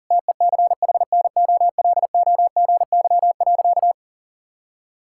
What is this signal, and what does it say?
Morse code, heard as a keyed tone, about 40 wpm, 700 Hz. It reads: TECHNOLOGY.
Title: Sentences from Top 1000 Words - Review Part 5 - 40wpm